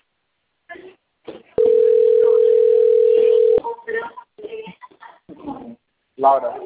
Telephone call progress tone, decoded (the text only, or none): ringback tone